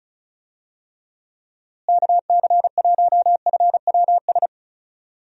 Morse code, heard as a keyed tone, about 36 wpm, 700 Hz. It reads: KC1FWS
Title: USA Call Signs - 6 Characters - Repeated 2x - 35wpm